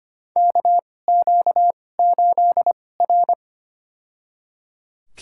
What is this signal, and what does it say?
Morse code, heard as a keyed tone, about 25 wpm, spaced as for 18 wpm, 700 Hz. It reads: KQ8R
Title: USA Call Signs - 2x Character Spacing - 25wpm